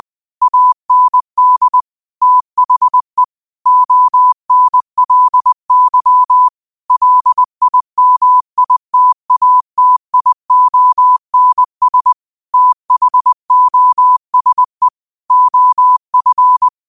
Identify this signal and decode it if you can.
Morse code, heard as a keyed tone, about 20 wpm, 1000 Hz. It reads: AND THE ONLY LIMITATIONS THOSE OF
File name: qotdfiles / 20wpm